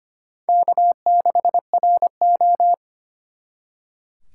Morse code, heard as a keyed tone, about 32 wpm, 700 Hz. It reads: K6RO